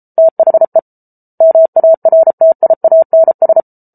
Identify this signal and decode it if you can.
Morse code, heard as a keyed tone, about 34 wpm, 650 Hz. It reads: THE MARTIANS